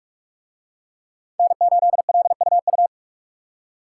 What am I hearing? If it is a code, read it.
Morse code, heard as a keyed tone, about 43 wpm, 700 Hz. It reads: N8LUU